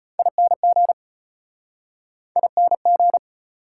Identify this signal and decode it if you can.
Morse code, heard as a keyed tone, about 33 wpm, 700 Hz. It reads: ING ING